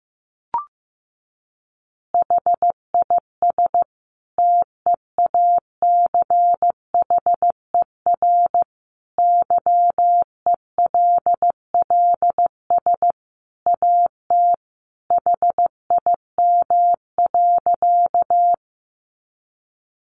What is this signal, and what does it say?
Morse code, heard as a keyed tone, about 15 wpm, 700 Hz. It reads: HIS TEACHER YELLS AT HIM.
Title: Additional Easy Phrases 15wpm